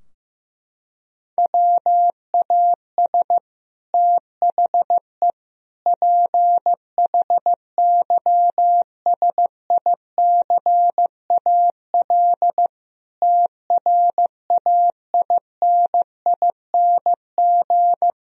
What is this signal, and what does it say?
Morse code, heard as a keyed tone, about 16 wpm, 700 Hz. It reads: WAS THE PHYSICAL TRAINING
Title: Sentences from Top 1000 Words - Review Part 4 - 15wpm